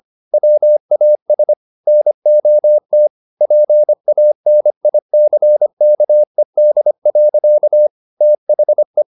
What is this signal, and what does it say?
Morse code, heard as a keyed tone, about 25 wpm, 600 Hz. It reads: WAS NOT PANICKED. THE
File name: the old man and the sea part 1 25wpm